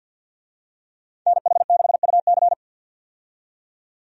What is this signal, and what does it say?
Morse code, heard as a keyed tone, about 51 wpm, 700 Hz. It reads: NH6UC
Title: USA Call Signs - 5 Characters - 50wpm